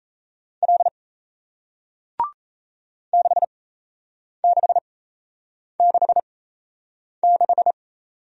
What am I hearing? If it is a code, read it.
Morse code, heard as a keyed tone, about 38 wpm, spaced as for 8 wpm, 700 Hz. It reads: L 6666